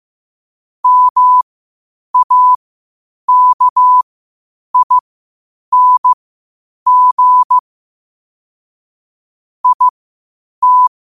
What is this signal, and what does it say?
Morse code, heard as a keyed tone, about 15 wpm, spaced as for 8 wpm, 1000 Hz. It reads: MAKING IT
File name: qotdfiles / 8wpm